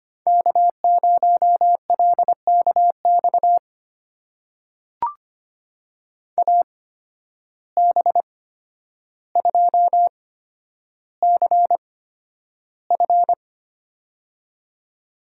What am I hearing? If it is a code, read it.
Morse code, heard as a keyed tone, about 25 wpm, spaced as for 11 wpm, 700 Hz. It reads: K0LKX AB2CF
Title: USA Call Signs - 5 Characters - 8x Character Spacing - 25wpm